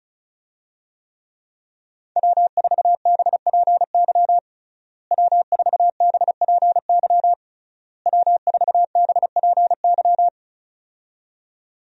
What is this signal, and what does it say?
Morse code, heard as a keyed tone, about 36 wpm, 700 Hz. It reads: W4BPY W4BPY W4BPY
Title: USA Call Signs - 5 Characters - Repeated 3x - 35wpm